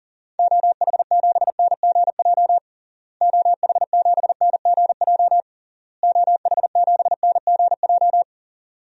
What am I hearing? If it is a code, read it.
Morse code, heard as a keyed tone, about 40 wpm, 700 Hz. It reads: OH7NGJ OH7NGJ OH7NGJ
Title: DXCC-100 Call Signs - 6 Characters - Repeated 3x - 40wpm